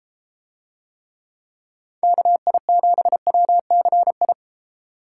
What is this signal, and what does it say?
Morse code, heard as a keyed tone, about 32 wpm, 700 Hz. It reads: KI7WCI